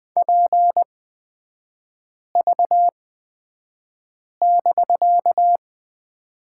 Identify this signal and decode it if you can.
Morse code, heard as a keyed tone, about 20 wpm, spaced as for 5 wpm, 700 Hz. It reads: PV<BK>
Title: 250 Lesson 40e - BK - Three Characters 20wpm